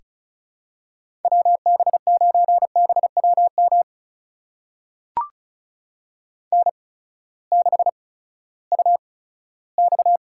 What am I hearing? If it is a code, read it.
Morse code, heard as a keyed tone, about 36 wpm, spaced as for 18 wpm, 700 Hz. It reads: WB9BWM N6UX
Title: USA Call Signs - 8x Character Spacing - 35wpm